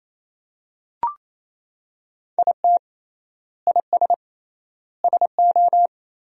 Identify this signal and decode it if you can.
Morse code, heard as a keyed tone, about 27 wpm, 700 Hz. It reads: IT IS SO
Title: 3x Word-Spacing - Sentences from Top 300 Words - 28wpm